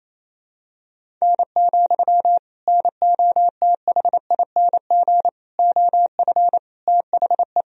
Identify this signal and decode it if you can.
Morse code, heard as a keyed tone, about 28 wpm, 700 Hz. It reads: N, NOTHING OF THE
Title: Sentences from Top 500 Words and Abbreviations 28wpm